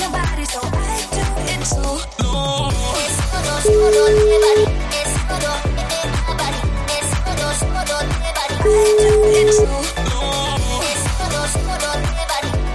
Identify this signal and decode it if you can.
Ringback tone (Europe, 425 Hz), with noise.